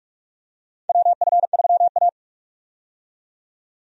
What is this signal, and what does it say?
Morse code, heard as a keyed tone, about 43 wpm, 700 Hz. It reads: WF3A
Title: USA Call Signs - 4 Characters - 45wpm